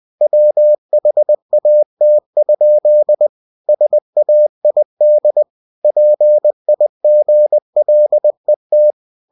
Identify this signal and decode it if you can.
Morse code, heard as a keyed tone, about 20 wpm, 600 Hz. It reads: WHAT? SAID PIGLET
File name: cw-books / winnie-the-pooh / winnie-20wpm-page-0079